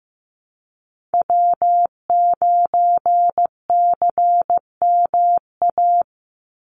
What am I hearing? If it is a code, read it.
Morse code, heard as a keyed tone, about 15 wpm, 700 Hz. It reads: W9CMA
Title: USA Call Signs 15wpm